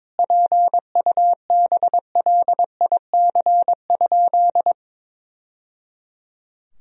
Morse code, heard as a keyed tone, about 22 wpm, 700 Hz. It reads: PUBLIC?
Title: Sentences from Top 1000 Words - Review Part 3 - 22wpm